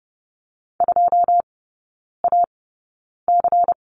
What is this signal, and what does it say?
Morse code, heard as a keyed tone, about 30 wpm, spaced as for 9 wpm, 700 Hz. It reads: 2AC
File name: Lesson 16e - Letter c - Three Characters 30wpm